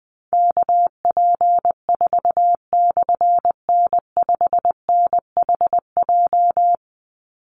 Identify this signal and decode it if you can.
Morse code, heard as a keyed tone, about 20 wpm, 700 Hz. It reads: KP4/N5NHJ
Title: Call Signs Repeated 3x - 20wpm